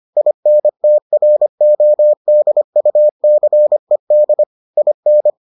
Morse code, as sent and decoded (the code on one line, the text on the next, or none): .. -. - .-. --- -.. ..- -.-. -.. / .. -.
INTRODUCED IN